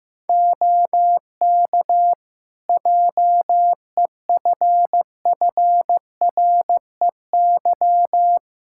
Morse code, heard as a keyed tone, about 15 wpm, 700 Hz. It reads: OK JEFFREY
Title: Practice QSO - Exchange 4 - Ending - 15wpm